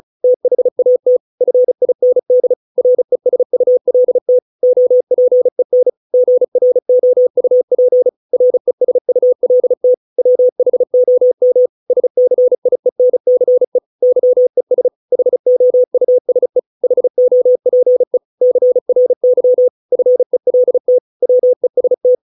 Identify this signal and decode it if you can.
Morse code, heard as a keyed tone, about 34 wpm, 500 Hz. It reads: THAT FIND RESULT OPEN GROUP RESULT WHOM SCIENCE YES HOUSE HOPE CRY FELT WEST